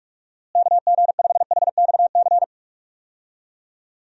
Morse code, heard as a keyed tone, about 46 wpm, 700 Hz. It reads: KG5HXC
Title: USA Call Signs - 6 Characters - Repeated 3x - 45wpm